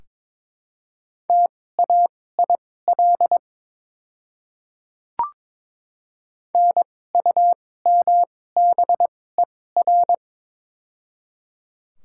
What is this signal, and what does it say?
Morse code, heard as a keyed tone, about 22 wpm, spaced as for 16 wpm, 700 Hz. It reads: TAIL NUMBER
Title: 2x Character Spacing - Top 500 Words - 22wpm